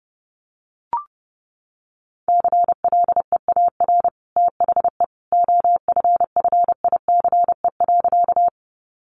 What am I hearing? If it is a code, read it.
Morse code, heard as a keyed tone, about 30 wpm, 700 Hz. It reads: CLEAR THE OFFICE.
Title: Sentences from Top 1000 Words 30wpm